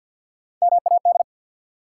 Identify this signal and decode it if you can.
Morse code, heard as a keyed tone, about 46 wpm, 700 Hz. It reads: MAD
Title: Three Letter Words - 50wpm